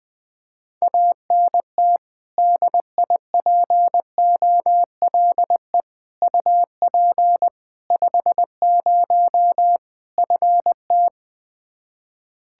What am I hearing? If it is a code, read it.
Morse code, heard as a keyed tone, about 20 wpm, 700 Hz. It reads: ANT DIPOLE UP 50 FT
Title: K1USN SST QSO Practice 20wpm